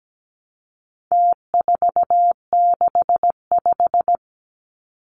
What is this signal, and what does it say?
Morse code, heard as a keyed tone, about 17 wpm, 700 Hz. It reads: T465
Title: Numbers - Contest Style - 4-Digit 17wpm